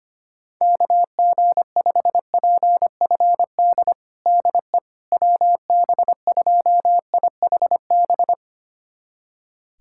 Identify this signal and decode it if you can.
Morse code, heard as a keyed tone, about 25 wpm, 700 Hz. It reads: KG5PFD DE WB2IHB
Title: USA Callsign DE Callsign 25wpm